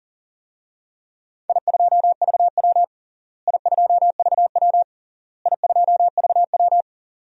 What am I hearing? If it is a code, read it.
Morse code, heard as a keyed tone, about 37 wpm, 700 Hz. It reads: I2VW I2VW I2VW
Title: DXCC-100 Call Signs - 4 Characters - Repeated 3x - 40wpm